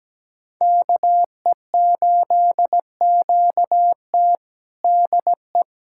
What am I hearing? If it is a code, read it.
Morse code, heard as a keyed tone, about 17 wpm, 700 Hz. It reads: KE8QT DE
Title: USA Callsign DE Callsign - 5 Characters - 17wpm